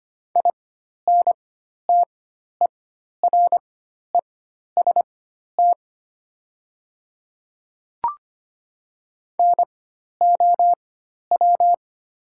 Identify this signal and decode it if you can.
Morse code, heard as a keyed tone, about 25 wpm, spaced as for 12 wpm, 700 Hz. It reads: INTEREST NOW
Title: 4x Character Spacing - Top 500 Words - 25wpm